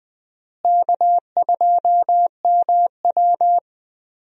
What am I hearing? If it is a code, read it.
Morse code, heard as a keyed tone, about 20 wpm, 700 Hz. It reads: K2MW
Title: USA Call Signs - 4 Characters - Repeated 2x - 20wpm